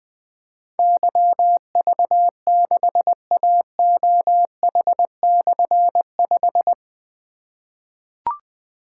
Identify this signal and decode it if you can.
Morse code, heard as a keyed tone, about 20 wpm, 700 Hz. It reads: YV6AOH/5
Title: Difficult Call Signs 20wpm